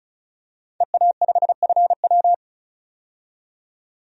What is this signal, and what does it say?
Morse code, heard as a keyed tone, about 36 wpm, 700 Hz. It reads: EA5FW
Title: DXCC-100 Call Signs - 5 Characters - 35wpm